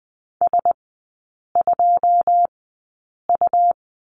Morse code, heard as a keyed tone, about 20 wpm, spaced as for 8 wpm, 690 Hz. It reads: S2U
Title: 212 Lesson 34e - Number 8 - Three Characters 20wpm